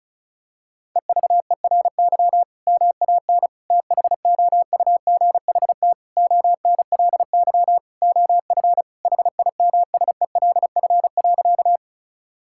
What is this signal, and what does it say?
Morse code, heard as a keyed tone, about 34 wpm, 700 Hz. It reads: EVERY MAN THOUGHT ONLY OF HIMSELF.
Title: Sentences from Top 1600 Words 35wpm